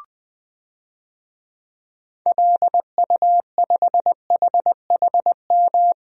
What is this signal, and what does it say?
Morse code, heard as a keyed tone, about 21 wpm, 700 Hz. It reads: LU5HHM
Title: DXCC-100 Call Signs - 6 Characters - 20wpm